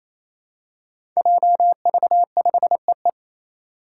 Morse code, heard as a keyed tone, about 28 wpm, 700 Hz. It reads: JV5EE
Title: Five Challenging Letter-Number Combinations 28wpm